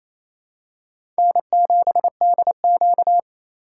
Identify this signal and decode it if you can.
Morse code, heard as a keyed tone, about 28 wpm, 700 Hz. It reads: N7DQ
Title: USA Call Signs - 4 Characters - 8x Character Spacing - 28wpm